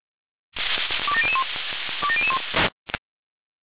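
Five-tone selective call (ZVEI), heard as quick runs of five tones, tones of about 70 ms. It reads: 27001 27001
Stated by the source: ZVEI five-tone selective call received on an SDR.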